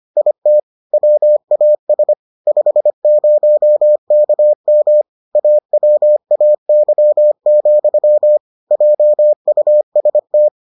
Morse code, heard as a keyed tone, about 25 wpm, 600 Hz. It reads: IT WAS 50KM AWAY, JUST